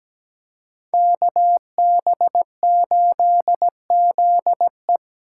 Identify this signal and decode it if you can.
Morse code, heard as a keyed tone, about 20 wpm, 700 Hz. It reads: KB8ZE
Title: Speed Racing - USA 2x2 Call Signs 17wpm